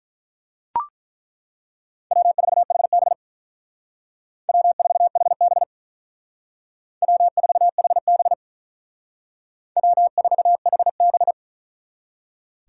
Morse code, heard as a keyed tone, about 42 wpm, 700 Hz. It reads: W4HB W4HB W4HB W4HB
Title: Speed Racing - USA 1x2 Call Signs 35wpm